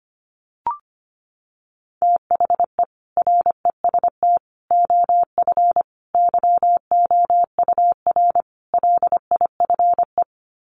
Morse code, heard as a keyed tone, about 25 wpm, 700 Hz. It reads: THE REST OF YOUR LIFE
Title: Sets of 5 Words 25wpm